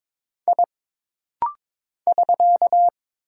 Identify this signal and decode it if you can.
Morse code, heard as a keyed tone, about 22 wpm, 700 Hz. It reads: I <SK>
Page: This rapid-fire timing is geared to push your ICR (Instant Character Recognition) skills.